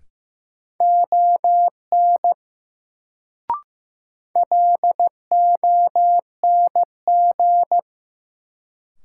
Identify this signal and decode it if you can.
Morse code, heard as a keyed tone, about 15 wpm, 700 Hz. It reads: ON LONG
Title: Sentences from Top 100 Words - New Word Drill - 15wpm